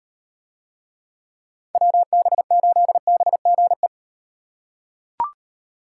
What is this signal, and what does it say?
Morse code, heard as a keyed tone, about 37 wpm, 700 Hz. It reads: WB8BGE